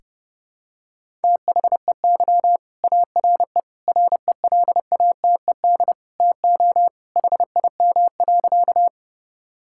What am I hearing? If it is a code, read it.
Morse code, heard as a keyed tone, about 29 wpm, 700 Hz. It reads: THEY ARE RELATED TO HIM.
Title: Sentences from Top 1000 Words - Review Part 5 - 30wpm